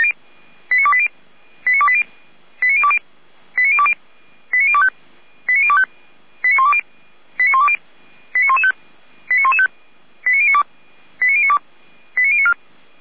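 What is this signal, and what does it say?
Five-tone selective call (ZVEI), heard as quick runs of five tones, tones of about 70 ms. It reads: 89280 89280 89020 89020 89025 89025 89120 89115 89115 89092 89092 89094
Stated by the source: ZVEI five-tone selective call, code 89262, received by SDR; fire brigade Feuerwehr Orlamünde.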